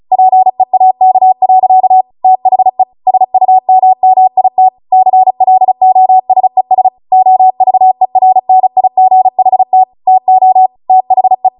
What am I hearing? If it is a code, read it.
Morse code, heard as a keyed tone, about 36 wpm, 750 Hz. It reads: PEAK. THE SUMMIT CLOSES OVERNIGHT TO THE
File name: Morse / 35WPM